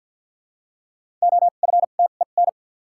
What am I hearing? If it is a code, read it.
Morse code, heard as a keyed tone, about 53 wpm, spaced as for 37 wpm, 700 Hz. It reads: OFTEN
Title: 2x Character Spacing - Top 300 Words - 50wpm